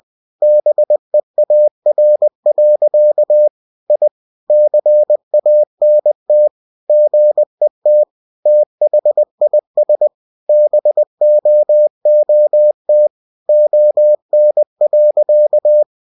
Morse code, heard as a keyed tone, about 20 wpm, 600 Hz. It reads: BEAR. I CANT GET THIS BOOT ON.
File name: cw-books / winnie-the-pooh / winnie-20wpm-page-0076